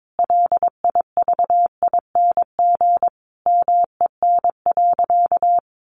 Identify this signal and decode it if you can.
Morse code, heard as a keyed tone, about 22 wpm, 700 Hz. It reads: LIVING MEN.
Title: Sentences from Top 1300 Words 22wpm